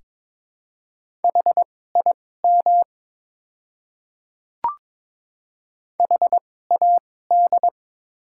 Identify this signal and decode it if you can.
Morse code, heard as a keyed tone, about 22 wpm, spaced as for 16 wpm, 700 Hz. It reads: HIM HAD
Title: 2x Character Spacing - Top 100 Words - 22wpm